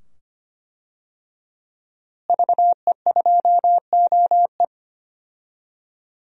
Morse code, heard as a keyed tone, about 25 wpm, 700 Hz. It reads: VE2OE
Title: DXCC-100 Call Signs - 5 Characters - Repeated 2x - 25wpm